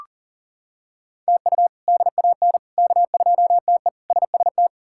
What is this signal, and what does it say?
Morse code, heard as a keyed tone, about 40 wpm, 700 Hz. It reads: TU DAN K2TE SST
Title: K1USN SST QSO Practice - 40wpm